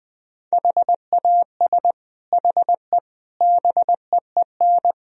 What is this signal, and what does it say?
Morse code, heard as a keyed tone, about 21 wpm, 700 Hz. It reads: HAS HE BEEN
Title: Sentences from Top 1000 Words - Review Part 1 - 20wpm